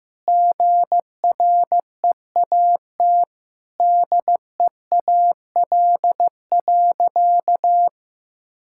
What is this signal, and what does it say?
Morse code, heard as a keyed tone, about 15 wpm, 700 Hz. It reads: GREAT DEAL.
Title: Sentences from Top 1000 Words - Review Part 6 - 15wpm